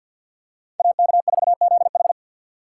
Morse code, heard as a keyed tone, about 53 wpm, 700 Hz. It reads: AK4ZH
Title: USA Call Signs - 5 Characters - 3x Character Spacing - 50wpm